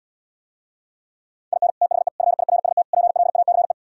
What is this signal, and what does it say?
Morse code, heard as a keyed tone, about 79 wpm, 700 Hz. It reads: IN THE FIRST PLACE